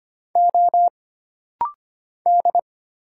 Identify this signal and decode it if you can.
Morse code, heard as a keyed tone, about 25 wpm, 700 Hz. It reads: O D
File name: Single Character (Letter:Number:Punctuation:Prosign) - Rapid-Fire - 25wpm